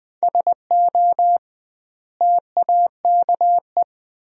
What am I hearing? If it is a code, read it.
Morse code, heard as a keyed tone, about 20 wpm, 700 Hz. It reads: SO TAKE
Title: 2x Word-Spacing - Sentences from Top 300 Words - 20wpm